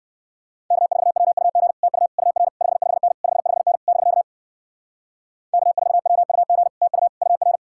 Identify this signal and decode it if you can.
Morse code, heard as a keyed tone, about 69 wpm, 700 Hz. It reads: K4WUG TU UR 55N 55N <BK> K4WUG TU UR